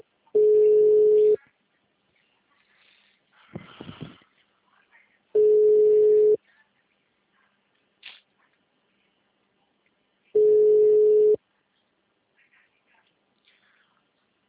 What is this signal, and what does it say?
Ringback tone (Europe, 425 Hz).